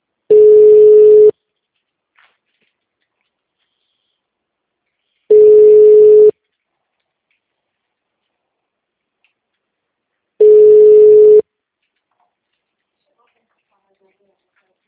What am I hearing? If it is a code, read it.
Ringback tone (Europe, 425 Hz), with speech.